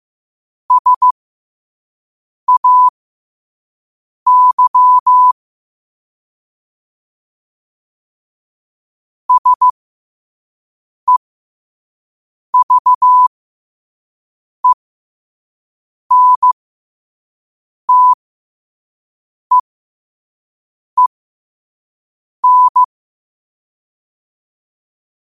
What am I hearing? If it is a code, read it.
Morse code, heard as a keyed tone, about 15 wpm, spaced as for 5 wpm, 1000 Hz. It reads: SAY SEVENTEEN
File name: qotdfiles / 5wpm